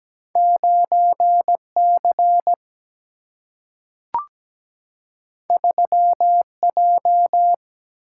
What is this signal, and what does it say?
Morse code, heard as a keyed tone, about 17 wpm, 700 Hz. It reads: 9 C 3 J
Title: Single-Letter Single-Number 17wpm